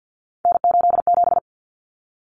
Morse code, heard as a keyed tone, about 50 wpm, 700 Hz. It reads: N87
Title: Common Cut Numbers - 3-Digit 50wpm